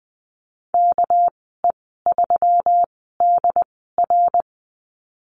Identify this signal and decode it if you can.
Morse code, heard as a keyed tone, about 20 wpm, spaced as for 14 wpm, 700 Hz. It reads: KE3DR
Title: 219 Lesson 35g - Number 0 - Call Signs 20wpm